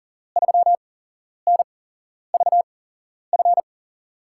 Morse code, heard as a keyed tone, about 40 wpm, spaced as for 11 wpm, 700 Hz. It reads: 3NVF
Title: USA Call Signs - 6 Characters - 8x Character Spacing - 40wpm